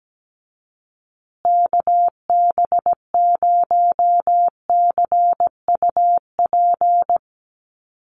Morse code, heard as a keyed tone, about 17 wpm, 700 Hz. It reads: KB0CUP
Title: USA Call Signs Repeated 3x 17wpm